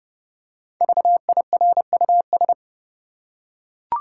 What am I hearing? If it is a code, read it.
Morse code, heard as a keyed tone, about 30 wpm, 700 Hz. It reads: VIRUS
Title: Five Letter Words - 30wpm